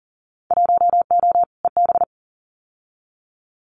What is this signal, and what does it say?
Morse code, heard as a keyed tone, about 39 wpm, 700 Hz. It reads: 1O EB
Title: Winter Field Day Exchanges 40wpm